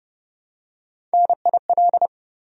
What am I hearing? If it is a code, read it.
Morse code, heard as a keyed tone, about 28 wpm, 700 Hz. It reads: NIL